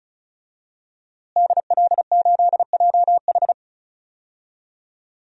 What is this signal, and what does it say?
Morse code, heard as a keyed tone, about 36 wpm, 700 Hz. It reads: DL8JH
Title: DXCC-100 Call Signs - 5 Characters - Repeated 3x - 35wpm